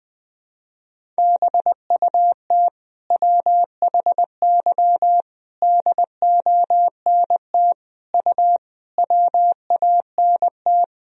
Morse code, heard as a keyed tone, about 21 wpm, 700 Hz. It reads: BUT WHY DONT U WANT
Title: Sentences from Top 300 Words and Abbreviations 20wpm